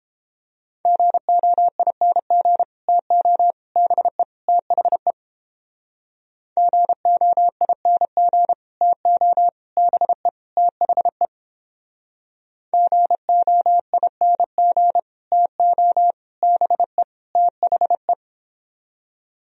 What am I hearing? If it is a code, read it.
Morse code, heard as a keyed tone, about 31 wpm, 700 Hz. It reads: GOING TO BE THE GOING TO BE THE GOING TO BE THE